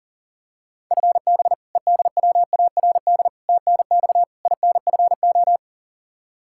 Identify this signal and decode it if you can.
Morse code, heard as a keyed tone, about 40 wpm, 700 Hz. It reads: FB EDWARD TNX INFO
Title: Practice QSO - Exchange 3 - More Info - 40wpm